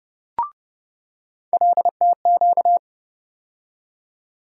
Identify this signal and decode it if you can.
Morse code, heard as a keyed tone, about 30 wpm, 700 Hz. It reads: LTQ
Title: USA Call Sign Postfixes - 3 Letters 30wpm